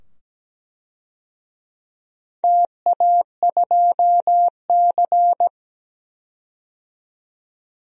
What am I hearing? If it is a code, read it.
Morse code, heard as a keyed tone, about 17 wpm, 700 Hz. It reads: TA2C
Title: DXCC-100 Call Signs - 4 Characters - 17wpm